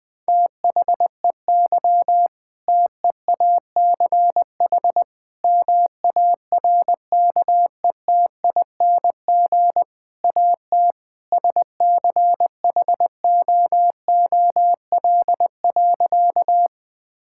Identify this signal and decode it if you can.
Morse code, heard as a keyed tone, about 20 wpm, 700 Hz. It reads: THEY TEACH MARKETING AT SCHOOL.
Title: Sentences from Top 1000 Words - Review Part 5 - 20wpm